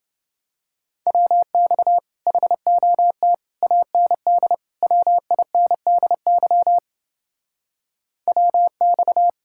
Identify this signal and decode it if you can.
Morse code, heard as a keyed tone, about 29 wpm, 700 Hz. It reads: WX HOT AND WINDY WX